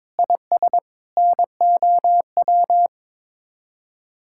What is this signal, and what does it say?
Morse code, heard as a keyed tone, about 22 wpm, 700 Hz. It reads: IS NOW
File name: practice / 22wpm / Sets of 8 Words - 22wpm